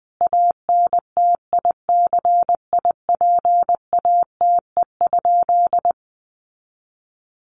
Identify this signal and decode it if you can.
Morse code, heard as a keyed tone, about 20 wpm, 700 Hz. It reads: ANTICIPATE?
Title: Sentences from Top 1500 Words 20wpm